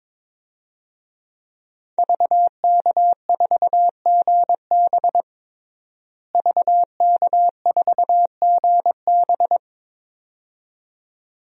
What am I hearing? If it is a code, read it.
Morse code, heard as a keyed tone, about 22 wpm, 700 Hz. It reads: VK4GB VK4GB
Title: DXCC-100 Call Signs - 5 Characters - Repeated 2x - 22wpm